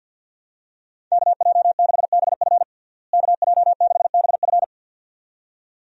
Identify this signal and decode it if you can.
Morse code, heard as a keyed tone, about 51 wpm, 700 Hz. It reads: KJ6BF KJ6BF
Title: USA Call Signs - 5 Characters - Repeated 2x - 50wpm